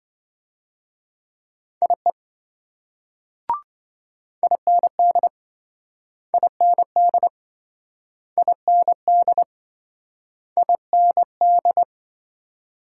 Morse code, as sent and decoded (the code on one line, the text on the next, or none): .. / .. -. -.. / .. -. -.. / .. -. -.. / .. -. -..
IE IND IND IND IND